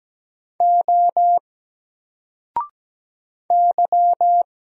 Morse code, heard as a keyed tone, about 17 wpm, 700 Hz. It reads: O Y
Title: Single Letter-Number 17wpm